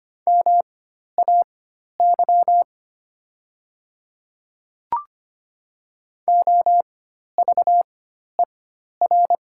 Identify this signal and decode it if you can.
Morse code, heard as a keyed tone, about 25 wpm, spaced as for 12 wpm, 700 Hz. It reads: MAY OVER
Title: 4x Character Spacing - Top 100 Words - 25wpm